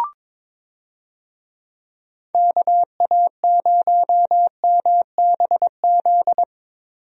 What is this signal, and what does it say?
Morse code, heard as a keyed tone, about 21 wpm, 700 Hz. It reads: KA0MBZ
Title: USA Call Signs - 6 Characters - Repeated 2x - 22wpm